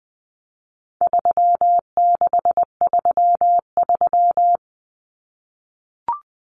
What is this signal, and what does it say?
Morse code, heard as a keyed tone, about 20 wpm, 700 Hz. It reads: 3633
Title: Numbers - 4-Digit 20wpm